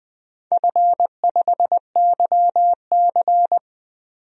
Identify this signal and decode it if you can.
Morse code, heard as a keyed tone, about 20 wpm, 700 Hz. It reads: F5YC
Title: DXCC-100 Call Signs - 4 Characters - 20wpm